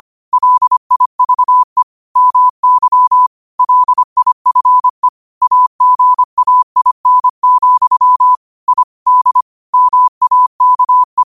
Morse code, heard as a keyed tone, about 25 wpm, 1000 Hz. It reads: LIVE MY LIFE AGAIN, I D MAKE
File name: qotdfiles / 25wpm